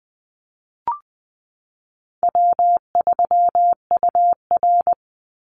Morse code, heard as a keyed tone, about 20 wpm, 700 Hz. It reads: W3UR
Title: Call Signs Repeated 3x - 20wpm